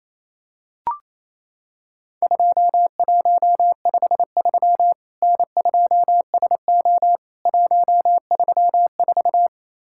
Morse code, heard as a keyed tone, about 28 wpm, 700 Hz. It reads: 2153 N2SO 134
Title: Worked All Europe DX Contest - QTC Exchange - 28wpm